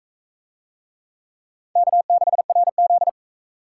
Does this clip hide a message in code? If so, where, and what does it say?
Yes: Morse code from 1.8 to 3.1 s, heard as a keyed tone, about 40 wpm, 700 Hz. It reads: K6RZ